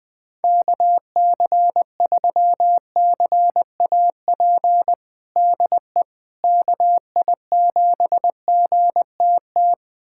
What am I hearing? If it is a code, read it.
Morse code, heard as a keyed tone, about 20 wpm, 700 Hz. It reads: KC3CAP DE KI7GTT
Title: USA Callsign DE Callsign 20wpm